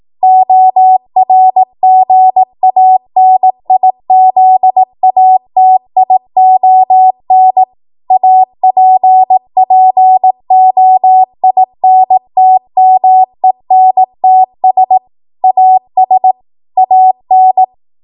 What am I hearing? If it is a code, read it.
Morse code, heard as a keyed tone, about 18 wpm, 750 Hz. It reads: ORGANIZATION APPOINTMENTS AS AN
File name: Archive / 18 WPM / 18WPM